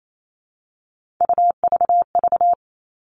Morse code, heard as a keyed tone, about 28 wpm, 700 Hz. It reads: UVV
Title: Three Challenging Letter-Number Combinations 28wpm